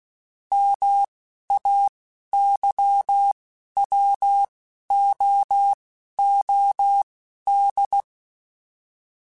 Morse code, heard as a keyed tone, about 16 wpm, spaced as for 11 wpm, 800 Hz. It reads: MAYWOOD
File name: fromSMs / FromSMs mp3s / Cities@10